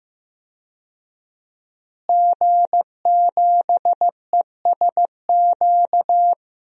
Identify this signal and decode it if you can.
Morse code, heard as a keyed tone, about 15 wpm, 700 Hz. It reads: G7ESQ
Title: DXCC-100 Call Signs - 5 Characters - Repeated 3x - 15wpm